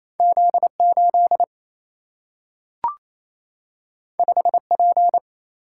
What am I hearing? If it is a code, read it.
Morse code, heard as a keyed tone, about 28 wpm, 700 Hz. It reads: Z8 5P
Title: Single-Letter Single-Number 28wpm